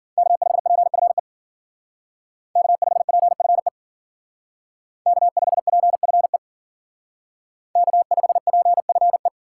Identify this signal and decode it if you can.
Morse code, heard as a keyed tone, about 52 wpm, 700 Hz. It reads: K5PFE K5PFE K5PFE K5PFE